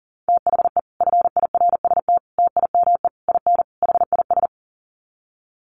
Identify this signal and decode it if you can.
Morse code, heard as a keyed tone, about 40 wpm, 700 Hz. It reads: THE FIRST TIME IN HIS
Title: Sets of 5 Words Encore2 40wpm